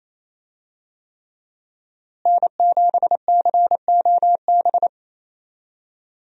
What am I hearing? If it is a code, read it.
Morse code, heard as a keyed tone, about 28 wpm, 700 Hz. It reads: N7COB
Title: USA Call Signs - 5 Characters - Repeated 3x - 28wpm